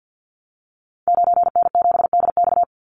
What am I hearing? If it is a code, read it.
Morse code, heard as a keyed tone, about 56 wpm, 700 Hz. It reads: 9N7DX
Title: Call Signs Repeated 2x - 50wpm